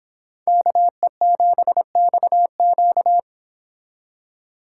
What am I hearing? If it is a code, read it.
Morse code, heard as a keyed tone, about 25 wpm, 700 Hz. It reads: KE7XQ